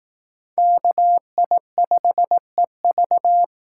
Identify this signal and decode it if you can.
Morse code, heard as a keyed tone, about 19 wpm, 700 Hz. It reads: KI5EV
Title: Speed Racing - USA 2x2 Call Signs 15wpm